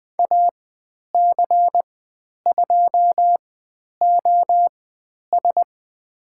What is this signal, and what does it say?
Morse code, heard as a keyed tone, about 20 wpm, spaced as for 10 wpm, 700 Hz. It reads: AC2OS
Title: 095 Lesson 16g - Letter c - Call Signs 20wpm